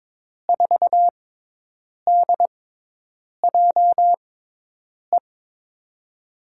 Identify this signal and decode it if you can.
Morse code, heard as a keyed tone, about 22 wpm, spaced as for 8 wpm, 700 Hz. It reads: 4DJE
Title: USA Call Signs - 6 Characters - 6x Character Spacing - 22wpm